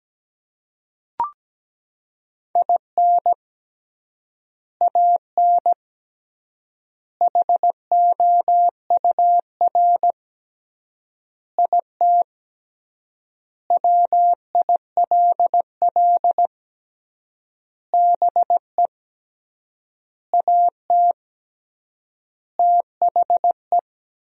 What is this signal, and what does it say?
Morse code, heard as a keyed tone, about 17 wpm, 700 Hz. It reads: IN AN HOUR IT WILL BE AT THE
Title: 3x Word-Spacing - Sentences from Top 400 Words - 17wpm